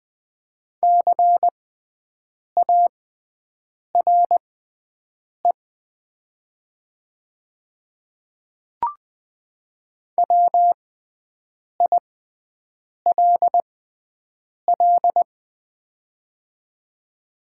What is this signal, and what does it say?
Morse code, heard as a keyed tone, about 20 wpm, spaced as for 7 wpm, 700 Hz. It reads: CARE WILL
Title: 6x Character Spacing - Top 500 Words - 20wpm